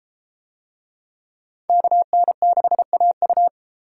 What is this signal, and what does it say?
Morse code, heard as a keyed tone, about 31 wpm, 700 Hz. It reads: KN6AU